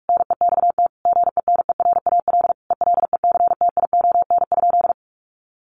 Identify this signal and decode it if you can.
Morse code, heard as a keyed tone, about 44 wpm, 700 Hz. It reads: NEXT GENERAL ELECTION?
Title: Sentences from Top 1500 Words 45wpm